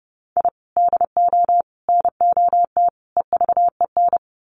Morse code, heard as a keyed tone, about 30 wpm, 700 Hz. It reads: I DO NOT EVEN
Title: Sets of 4 Words 30wpm